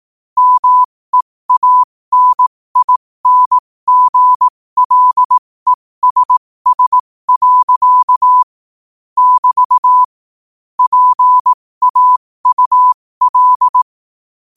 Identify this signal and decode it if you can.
Morse code, heard as a keyed tone, about 18 wpm, spaced as for 15 wpm, 1000 Hz. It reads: MEANINGLESS. <BT> PAUL
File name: qotdfiles / 15wpm